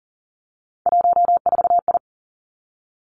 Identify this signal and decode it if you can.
Morse code, heard as a keyed tone, about 42 wpm, 700 Hz. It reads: 14I
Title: Three Challenging Letter-Number Combinations 40wpm